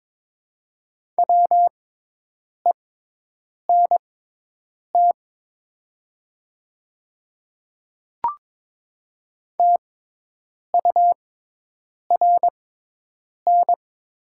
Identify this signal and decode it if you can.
Morse code, heard as a keyed tone, about 22 wpm, spaced as for 8 wpm, 700 Hz. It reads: WENT TURN
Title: 6x Character Spacing - Top 200 Words - 22wpm